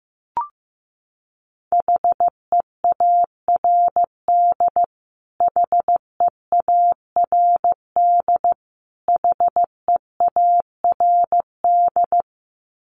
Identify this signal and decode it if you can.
Morse code, heard as a keyed tone, about 15 wpm, 700 Hz. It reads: HEARD HEARD HEARD
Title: Top 500 Words Repeated 15wpm